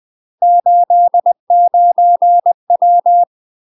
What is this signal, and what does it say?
Morse code, heard as a keyed tone, about 20 wpm, 700 Hz. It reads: 89W
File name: UK MASTER QSOs TRAINING / 20WPM / UK MASTER QSOs 0028